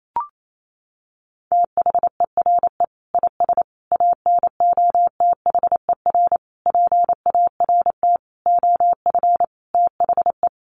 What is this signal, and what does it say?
Morse code, heard as a keyed tone, about 28 wpm, 700 Hz. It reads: THERE IS ANOTHER PART OF THE
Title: Sentences from Top 1300 Words 28wpm